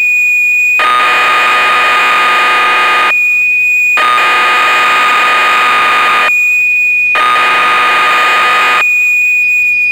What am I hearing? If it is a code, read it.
Cassette data (Kansas City type).
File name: magazines / Acorn User / 1985-06 BBC Programs